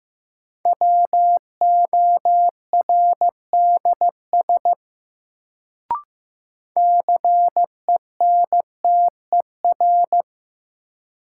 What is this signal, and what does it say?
Morse code, heard as a keyed tone, about 15 wpm, 700 Hz. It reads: WORDS CENTER
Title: Sets of 3 Words - New Word Drill - 15wpm